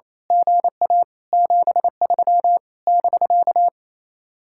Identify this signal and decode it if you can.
Morse code, heard as a keyed tone, about 26 wpm, 700 Hz. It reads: GA 73 <BK>
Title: Speed-Racing Common QSO Phrases - 28wpm